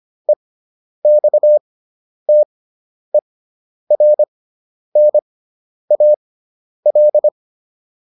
Morse code, heard as a keyed tone, about 25 wpm, spaced as for 10 wpm, 600 Hz. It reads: EXTERNAL